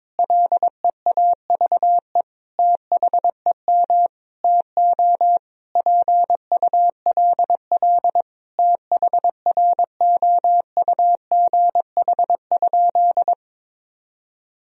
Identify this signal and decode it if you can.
Morse code, heard as a keyed tone, about 22 wpm, 700 Hz. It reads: LEAVE THEM TO PULL THROUGH?
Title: Sentences from Top 1000 Words - Review Part 2 - 22wpm